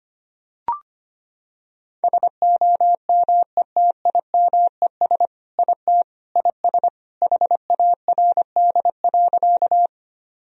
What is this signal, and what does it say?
Morse code, heard as a keyed tone, about 25 wpm, 700 Hz. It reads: SOMETIMES IT IS HARD.
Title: Sentences from Top 1000 Words - Review Part 3 - 25wpm